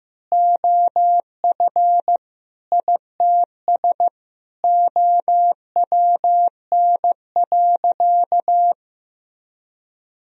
Morse code, heard as a keyed tone, about 15 wpm, 700 Hz. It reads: OF ITS OWN.
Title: Sentences from Top 1000 Words - Review Part 4 - 15wpm